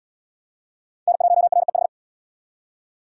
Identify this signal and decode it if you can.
Morse code, heard as a keyed tone, about 72 wpm, 700 Hz. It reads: N1REH